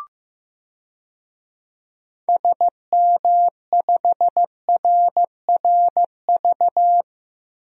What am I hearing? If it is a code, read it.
Morse code, heard as a keyed tone, about 15 wpm, 700 Hz. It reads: SM5RRV
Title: DXCC-100 Call Signs - 6 Characters - Repeated 3x - 15wpm